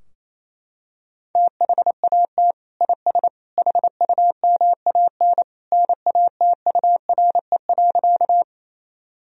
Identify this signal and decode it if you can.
Morse code, heard as a keyed tone, about 28 wpm, 700 Hz. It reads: THAT IS HUMAN NATURE.
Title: Sentences from Top 1000 Words - Review Part 3 - 28wpm